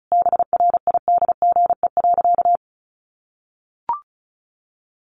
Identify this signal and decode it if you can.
Morse code, heard as a keyed tone, about 34 wpm, 700 Hz. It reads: BRIDGE.
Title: Sentences from Top 1400 Words 35wpm